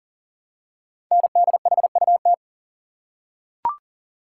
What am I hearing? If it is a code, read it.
Morse code, heard as a keyed tone, about 40 wpm, 700 Hz. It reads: NDHUT
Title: Code Groups - 5 Letters - 40wpm